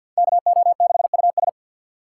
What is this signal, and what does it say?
Morse code, heard as a keyed tone, about 46 wpm, 700 Hz. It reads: KO6US
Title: USA Call Signs - 5 Characters - 6x Character Spacing - 50wpm